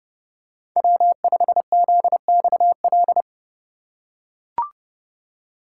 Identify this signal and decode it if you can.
Morse code, heard as a keyed tone, about 29 wpm, 700 Hz. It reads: W5ZXL